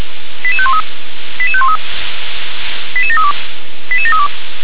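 Five-tone selective call (ZVEI), heard as quick runs of five tones, tones of about 70 ms. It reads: 88513 88513 88732 88732
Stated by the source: ZVEI five-tone selective call received via SDR.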